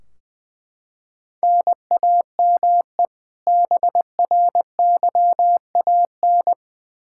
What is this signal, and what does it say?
Morse code, heard as a keyed tone, about 21 wpm, 700 Hz. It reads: NAME BRYAN
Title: K1USN SST QSO Practice 20wpm